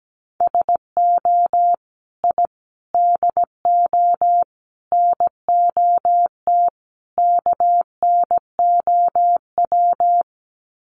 Morse code, heard as a keyed tone, about 17 wpm, 700 Hz. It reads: SO I DO NOT KNOW
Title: Sets of 5 Words Encore 17wpm